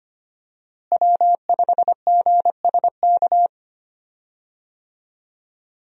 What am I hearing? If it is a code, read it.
Morse code, heard as a keyed tone, about 25 wpm, 700 Hz. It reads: W5GSK
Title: USA Call Signs - 5 Characters - Repeated 2x - 25wpm